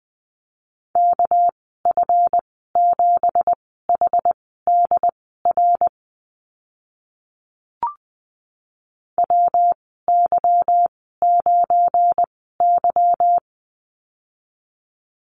Morse code, heard as a keyed tone, about 20 wpm, spaced as for 15 wpm, 700 Hz. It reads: KF7HDR WY9Y
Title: 200 Lesson 32g - Letter k - Call Signs 20wpm